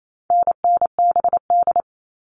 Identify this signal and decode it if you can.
Morse code, heard as a keyed tone, about 28 wpm, 700 Hz. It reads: NNBD
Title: All Cut Numbers - 4-Digit 28wpm